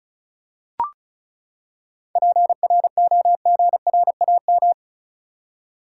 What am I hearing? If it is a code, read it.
Morse code, heard as a keyed tone, about 36 wpm, 700 Hz. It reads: PROGRAM
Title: Seven Letter Words - 35wpm